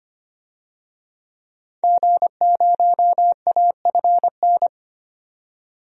Morse code, heard as a keyed tone, about 25 wpm, 700 Hz. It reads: G0AFN
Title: DXCC-100 Call Signs - 5 Characters - 25wpm